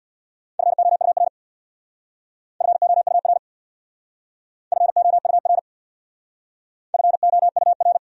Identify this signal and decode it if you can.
Morse code, heard as a keyed tone, about 64 wpm, 700 Hz. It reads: FOUR FOUR FOUR FOUR